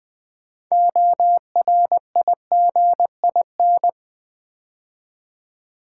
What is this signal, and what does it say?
Morse code, heard as a keyed tone, about 20 wpm, 700 Hz. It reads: ORIGIN
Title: Build A Word - 3 Letters At A Time - 9-Letter Words - 20wpm